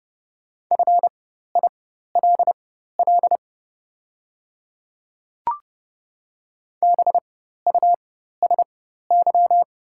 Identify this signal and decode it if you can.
Morse code, heard as a keyed tone, about 30 wpm, spaced as for 14 wpm, 700 Hz. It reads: FILL BUSY
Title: 4x Character Spacing - Top 500 Words - 30wpm